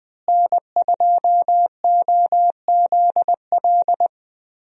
Morse code, heard as a keyed tone, about 20 wpm, 700 Hz. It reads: N2OZL
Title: USA Call Signs - 5 Characters - Repeated 3x - 20wpm